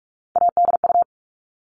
Morse code, heard as a keyed tone, about 44 wpm, 700 Hz. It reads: ADU
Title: All Cut Numbers - 3-Digit 45wpm